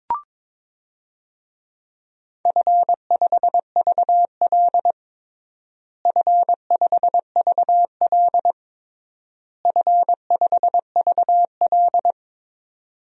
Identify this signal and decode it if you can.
Morse code, heard as a keyed tone, about 22 wpm, 700 Hz. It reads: F5VL F5VL F5VL
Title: DXCC-100 Call Signs - 4 Characters - Repeated 3x - 22wpm